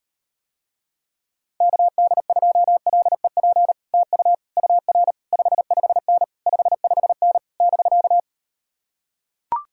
Morse code, heard as a keyed tone, about 39 wpm, 700 Hz. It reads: KD2LEP TU UR 55N 55N <BK>